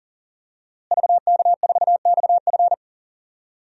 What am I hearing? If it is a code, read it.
Morse code, heard as a keyed tone, about 40 wpm, 700 Hz. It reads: VK4XF
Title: DXCC-100 Call Signs - 5 Characters - Repeated 2x - 40wpm